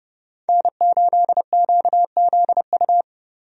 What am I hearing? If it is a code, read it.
Morse code, heard as a keyed tone, about 29 wpm, 700 Hz. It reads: N8QZU
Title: USA Call Signs - 5 Characters - Repeated 3x - 30wpm